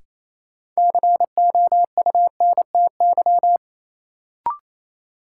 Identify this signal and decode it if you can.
Morse code, heard as a keyed tone, about 28 wpm, 700 Hz. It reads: COUNTY
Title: Sentences from Top 1200 Words - New Word Drill Alt - 28wpm